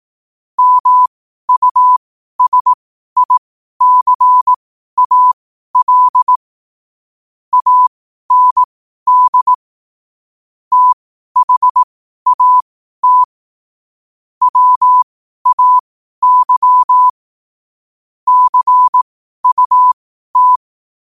Morse code, heard as a keyed tone, about 18 wpm, spaced as for 12 wpm, 1000 Hz. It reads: MUSICAL AND THAT WAY CUT
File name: qotdfiles / 12wpm